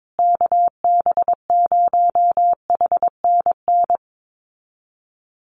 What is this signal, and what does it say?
Morse code, heard as a keyed tone, about 22 wpm, 700 Hz. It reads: KB0HNN
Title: USA Call Signs Repeated 2x 22wpm